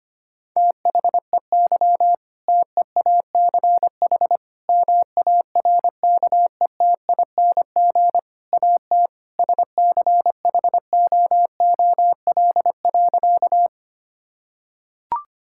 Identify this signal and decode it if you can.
Morse code, heard as a keyed tone, about 25 wpm, 700 Hz. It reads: THEY TEACH MARKETING AT SCHOOL.
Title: Sentences from Top 1000 Words - Review Part 5 - 25wpm